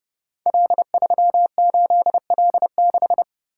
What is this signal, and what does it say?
Morse code, heard as a keyed tone, about 30 wpm, 700 Hz. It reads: L38L6
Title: Code Groups - 5 Letter:Numbers - 30wpm